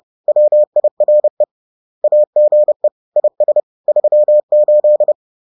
Morse code, heard as a keyed tone, about 30 wpm, 600 Hz. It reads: WIRE AGE IS 38
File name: 30wpm-4